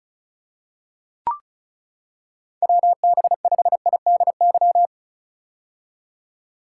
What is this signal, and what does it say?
Morse code, heard as a keyed tone, about 36 wpm, 700 Hz. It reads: WB5IDY
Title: K1USN SST QSO Practice - 35wpm